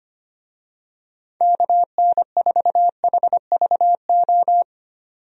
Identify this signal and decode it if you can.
Morse code, heard as a keyed tone, about 25 wpm, 700 Hz. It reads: KN4HVO
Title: USA Call Signs - 6 Characters - 25wpm